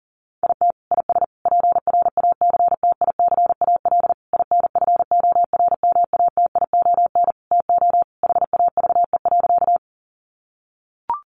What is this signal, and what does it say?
Morse code, heard as a keyed tone, about 40 wpm, 700 Hz. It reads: IT IS PRACTICAL INFORMATION TO HAVE.
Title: Sentences from Top 1400 Words 40wpm